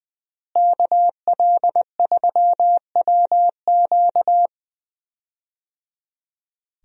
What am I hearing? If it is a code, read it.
Morse code, heard as a keyed tone, about 20 wpm, 700 Hz. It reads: KL3WQ
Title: USA Call Signs - 5 Characters - Repeated 3x - 20wpm